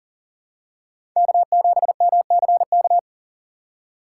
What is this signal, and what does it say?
Morse code, heard as a keyed tone, about 40 wpm, 700 Hz. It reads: K7MCK
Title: USA Call Signs - 5 Characters - 3x Character Spacing - 40wpm